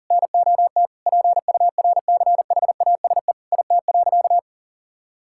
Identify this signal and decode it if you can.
Morse code, heard as a keyed tone, about 40 wpm, 700 Hz. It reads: NOT PURCHASE IT.
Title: Sentences from Top 1000 Words - Review Part 5 - 40wpm